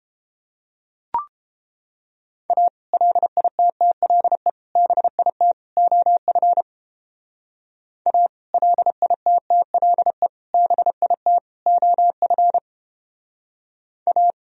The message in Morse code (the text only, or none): A LITTLE BIT OF A LITTLE BIT OF A